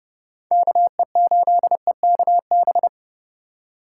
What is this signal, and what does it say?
Morse code, heard as a keyed tone, about 39 wpm, 700 Hz. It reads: KE8EKB